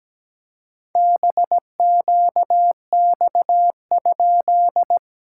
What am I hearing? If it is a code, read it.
Morse code, heard as a keyed tone, about 17 wpm, 700 Hz. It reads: BQX?
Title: SOTA QSO Practice 17wpm